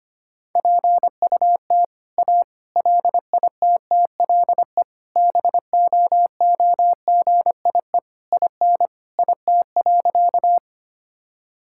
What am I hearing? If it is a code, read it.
Morse code, heard as a keyed tone, about 25 wpm, 700 Hz. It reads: PUT A LITTLE BOOGIE IN IT.
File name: practice / 25wpm / All-Age Appropriate Jokes - 25wpm